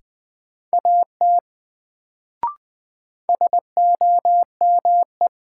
Morse code, heard as a keyed tone, about 20 wpm, 700 Hz. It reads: AT SOME
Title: Sets of 2 Words - New Word Drill - 20wpm